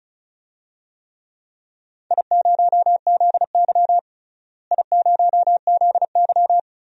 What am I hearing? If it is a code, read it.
Morse code, heard as a keyed tone, about 36 wpm, 700 Hz. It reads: I0ZY I0ZY
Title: DXCC-100 Call Signs - 4 Characters - Repeated 2x - 35wpm